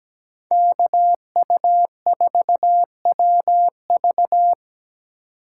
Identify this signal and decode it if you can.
Morse code, heard as a keyed tone, about 17 wpm, 700 Hz. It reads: KU4WV
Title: USA Call Signs - 5 Characters - 8x Character Spacing - 17wpm